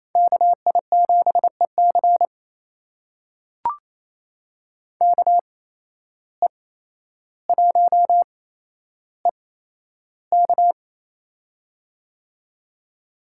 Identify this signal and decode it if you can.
Morse code, heard as a keyed tone, about 28 wpm, spaced as for 12 wpm, 700 Hz. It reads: KI7EC KE1EK
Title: USA Call Signs - 5 Characters - 8x Character Spacing - 28wpm